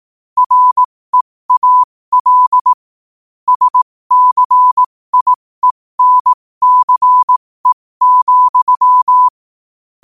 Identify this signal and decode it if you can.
Morse code, heard as a keyed tone, about 18 wpm, spaced as for 15 wpm, 1000 Hz. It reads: REAL SCIENCE,
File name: qotdfiles / 15wpm